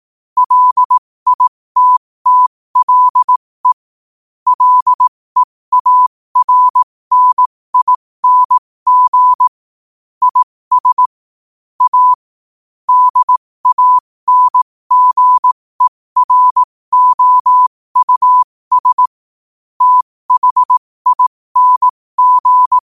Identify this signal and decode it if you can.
Morse code, heard as a keyed tone, about 18 wpm, spaced as for 15 wpm, 1000 Hz. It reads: LITTLE LEARNING IS A DANGEROUS THING
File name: qotdfiles / 15wpm